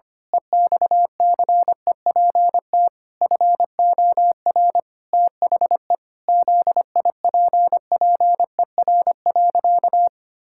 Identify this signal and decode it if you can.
Morse code, heard as a keyed tone, about 25 wpm, 700 Hz. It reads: EXCEPT FOR THE ZIPPER.
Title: Pangram Challenge v2 - 25wpm